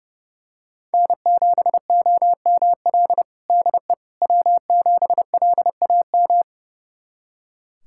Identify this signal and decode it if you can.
Morse code, heard as a keyed tone, about 30 wpm, 700 Hz. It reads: N7OML DE W7LAM
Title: USA Callsign DE Callsign - 5 Characters - 30wpm